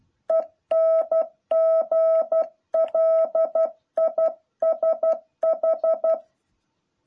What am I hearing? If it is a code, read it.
Morse code, heard as a keyed tone, about 12 wpm, 650 Hz. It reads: ENGLISH
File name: jh8cbh / 2021cw